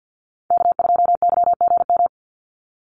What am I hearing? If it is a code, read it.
Morse code, heard as a keyed tone, about 50 wpm, 700 Hz. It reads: K2YGM
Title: Call Signs - 50wpm